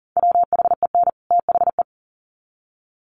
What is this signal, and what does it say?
Morse code, heard as a keyed tone, about 40 wpm, 700 Hz. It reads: WHEN THE
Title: Sets of 2 Words 40wpm